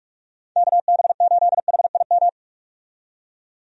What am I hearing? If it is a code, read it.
Morse code, heard as a keyed tone, about 43 wpm, 700 Hz. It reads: KB8HIM
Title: USA Call Signs - 6 Characters - 45wpm